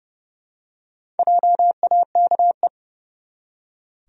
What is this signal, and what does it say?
Morse code, heard as a keyed tone, about 30 wpm, 700 Hz. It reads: JAKE
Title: USA Common Names - Max 5 Letters - 30wpm